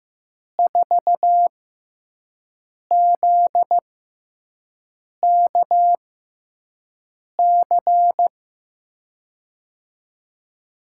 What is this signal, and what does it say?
Morse code, heard as a keyed tone, about 16 wpm, spaced as for 5 wpm, 700 Hz. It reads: 4ZKC
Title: USA Call Signs - 6 Characters - 6x Character Spacing - 15wpm